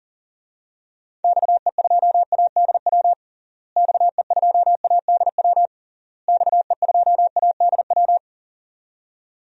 Morse code, heard as a keyed tone, about 40 wpm, 700 Hz. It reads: XE2ADW XE2ADW XE2ADW
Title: DXCC-100 Call Signs - 6 Characters - Repeated 3x - 40wpm